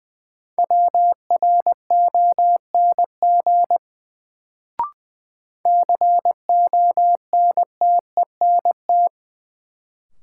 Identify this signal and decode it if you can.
Morse code, heard as a keyed tone, about 21 wpm, 700 Hz. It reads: WRONG CONTENT
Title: Sentences from Top 900 Words - New Word Drill - 20wpm